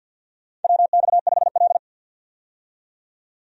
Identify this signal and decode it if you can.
Morse code, heard as a keyed tone, about 52 wpm, 700 Hz. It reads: WX5L
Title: USA Call Signs - 4 Characters - 50wpm